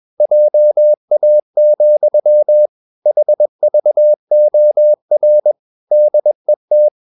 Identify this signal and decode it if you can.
Morse code, heard as a keyed tone, about 21 wpm, 600 Hz. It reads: JA, HVOR DET
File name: Ferie QRS 21wpm